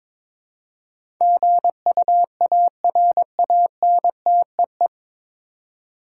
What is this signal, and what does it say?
Morse code, heard as a keyed tone, about 22 wpm, 700 Hz. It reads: GUARANTEE
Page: Morse Code Ninja practice 22wpm